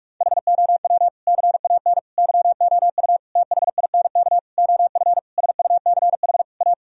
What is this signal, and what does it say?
Morse code, heard as a keyed tone, about 46 wpm, 700 Hz. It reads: HOW CAN YOU THINK OF SUCH A
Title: Sentences from Top 1000 Words - Review Part 3 - 45wpm